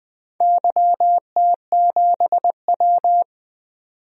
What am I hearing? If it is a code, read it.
Morse code, heard as a keyed tone, about 21 wpm, 700 Hz. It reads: YT7W
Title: DXCC-100 Call Signs - 4 Characters - 20wpm